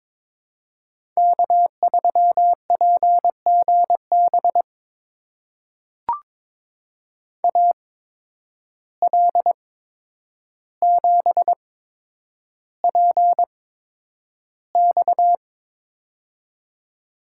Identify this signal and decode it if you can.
Morse code, heard as a keyed tone, about 22 wpm, spaced as for 9 wpm, 700 Hz. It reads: K3PGB AL7PX
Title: USA Call Signs - 5 Characters - 8x Character Spacing - 22wpm